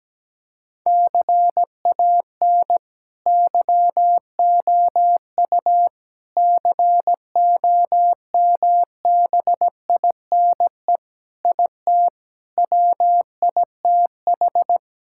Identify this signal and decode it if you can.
Morse code, heard as a keyed tone, about 17 wpm, 700 Hz. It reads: CAN YOU COMBINE IT WITH
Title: Sentences from Top 1600 Words 17wpm